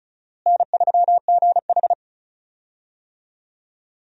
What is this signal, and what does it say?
Morse code, heard as a keyed tone, about 36 wpm, 700 Hz. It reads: N3GH
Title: USA Call Signs - 4 Characters - Repeated 3x - 35wpm